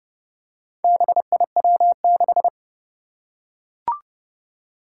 Morse code, heard as a keyed tone, about 30 wpm, 700 Hz. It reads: BIW6
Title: Four Challenging Letter-Number Combinations 30wpm